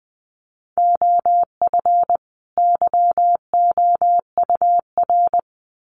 Morse code, heard as a keyed tone, about 20 wpm, 700 Hz. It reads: OF YOUR
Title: Sets of 2 Words 20wpm